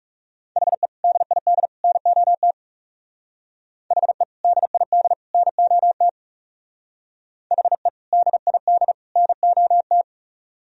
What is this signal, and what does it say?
Morse code, heard as a keyed tone, about 37 wpm, 700 Hz. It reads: HE DID NOT HE DID NOT HE DID NOT
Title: Speed-Racing - Sets of 3 Words 30wpm